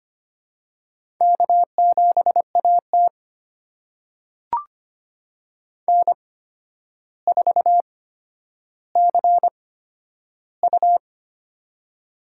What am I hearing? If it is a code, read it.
Morse code, heard as a keyed tone, about 25 wpm, spaced as for 11 wpm, 700 Hz. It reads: K7AT N4CU
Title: USA Call Signs - 4 Characters - 8x Character Spacing - 25wpm